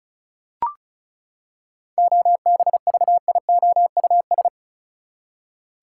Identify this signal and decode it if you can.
Morse code, heard as a keyed tone, about 34 wpm, 700 Hz. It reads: OBVIOUS
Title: Seven Letter Words - 35wpm